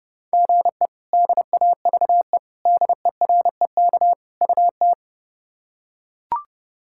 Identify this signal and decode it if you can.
Morse code, heard as a keyed tone, about 30 wpm, 700 Hz. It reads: GE DAVE DEREK UT
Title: K1USN SST QSO Practice - 30wpm